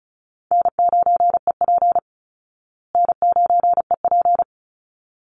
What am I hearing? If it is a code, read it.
Morse code, heard as a keyed tone, about 34 wpm, 700 Hz. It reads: N9EP N9EP
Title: Easy Call Signs Repeated 2x - 35wpm